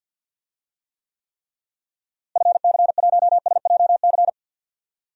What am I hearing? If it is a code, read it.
Morse code, heard as a keyed tone, about 52 wpm, 700 Hz. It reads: FC1SJC
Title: DXCC-100 Call Signs - 6 Characters - Repeated 3x - 50wpm